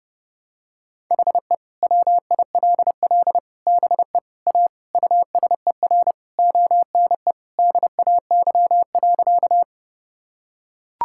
Morse code, heard as a keyed tone, about 32 wpm, 700 Hz. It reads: HE WILL BE A USER ONE DAY.
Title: Sentences from Top 1000 Words - Review Part 5 - 30wpm